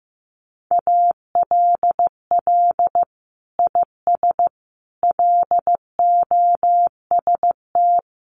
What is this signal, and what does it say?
Morse code, heard as a keyed tone, about 14 wpm, 700 Hz. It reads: ALL IS LOST
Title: Sentences from Top 1400 Words 15wpm